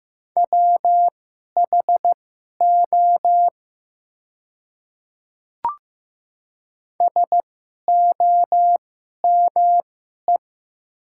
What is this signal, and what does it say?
Morse code, heard as a keyed tone, about 15 wpm, spaced as for 11 wpm, 700 Hz. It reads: WHO SOME
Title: 2x Character Spacing - Top 100 Words - 15wpm